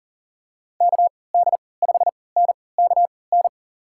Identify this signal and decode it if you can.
Morse code, heard as a keyed tone, about 40 wpm, spaced as for 23 wpm, 700 Hz. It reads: KD5NKN
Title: USA Call Signs - 6 Characters - 3x Character Spacing - 40wpm